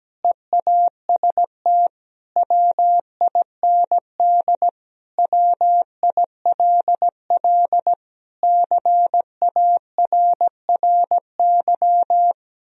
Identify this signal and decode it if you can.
Morse code, heard as a keyed tone, about 17 wpm, 700 Hz. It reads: EAST WIND WILL CARRY
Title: Sentences from Top 1000 Words - Review Part 3 - 17wpm